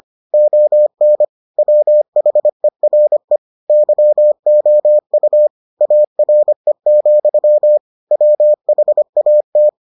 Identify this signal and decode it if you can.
Morse code, heard as a keyed tone, about 25 wpm, 600 Hz. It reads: ON WHERE YOU ARE, WHAT